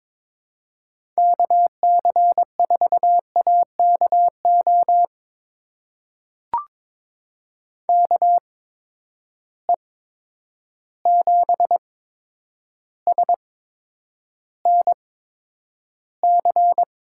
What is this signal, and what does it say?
Morse code, heard as a keyed tone, about 22 wpm, spaced as for 9 wpm, 700 Hz. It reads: KC4AKO KE7SNC
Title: USA Call Signs - 6 Characters - 8x Character Spacing - 22wpm